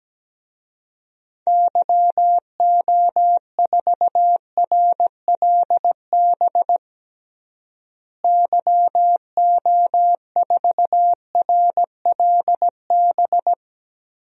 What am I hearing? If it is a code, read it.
Morse code, heard as a keyed tone, about 17 wpm, 700 Hz. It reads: YO4RLB YO4RLB
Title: DXCC-100 Call Signs - 6 Characters - Repeated 2x - 17wpm